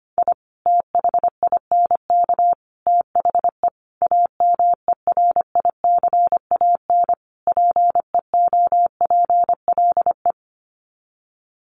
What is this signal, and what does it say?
Morse code, heard as a keyed tone, about 25 wpm, 700 Hz. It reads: I THINK THE AMERICAN PEOPLE
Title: Sets of 5 Words Encore 25wpm